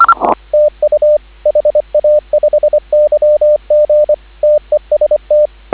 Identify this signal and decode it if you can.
Morse code, heard as a keyed tone, about 25 wpm, 590 Hz, with radio noise. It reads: TU HA5YG TEST